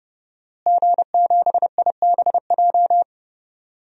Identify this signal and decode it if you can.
Morse code, heard as a keyed tone, about 30 wpm, 700 Hz. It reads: G7IBJ
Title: DXCC-100 Call Signs - 5 Characters - Repeated 3x - 30wpm